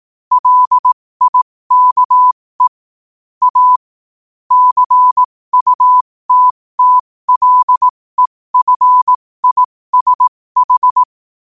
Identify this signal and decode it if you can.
Morse code, heard as a keyed tone, about 18 wpm, spaced as for 15 wpm, 1000 Hz. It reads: LIKE A CUTTLEFISH
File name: 15wpm